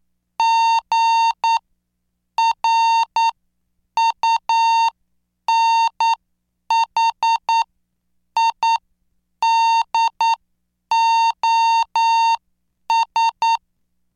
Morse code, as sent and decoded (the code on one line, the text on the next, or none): --. .-. ..- -. .... .. -.. --- ...
GRUNHIDOS